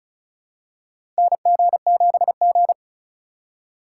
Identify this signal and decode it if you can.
Morse code, heard as a keyed tone, about 34 wpm, 700 Hz. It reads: NG7G
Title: USA Call Signs - 4 Characters - 4x Character Spacing - 35wpm